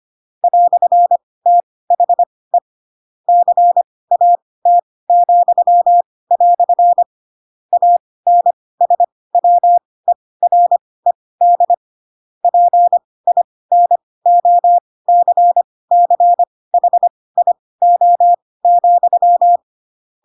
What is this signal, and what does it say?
Morse code, heard as a keyed tone, about 25 wpm, spaced as for 18 wpm, 700 Hz. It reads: "THE CAT," ANSWERED PINOCCHIO,
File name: Pinocchio / Pinocchio ch7 12 25 ik1hge